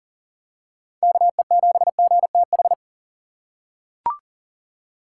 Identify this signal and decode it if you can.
Morse code, heard as a keyed tone, about 40 wpm, 700 Hz. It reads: KE7GTH
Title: USA Call Signs - 6 Characters - 3x Character Spacing - 40wpm